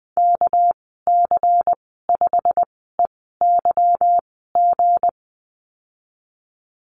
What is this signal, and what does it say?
Morse code, heard as a keyed tone, about 20 wpm, spaced as for 15 wpm, 700 Hz. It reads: KC5EYG
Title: 207 Lesson 33g - Letter j - Call Signs 20wpm